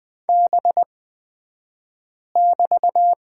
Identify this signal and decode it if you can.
Morse code, heard as a keyed tone, about 20 wpm, spaced as for 5 wpm, 700 Hz. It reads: B<BT>
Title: 254 Lesson 41d - Period - Two Characters 20wpm